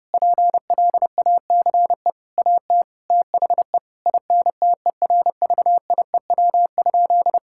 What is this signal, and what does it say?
Morse code, heard as a keyed tone, about 30 wpm, 700 Hz. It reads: PLACE AT THE INTERVIEW?
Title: Sentences from Top 1600 Words 30wpm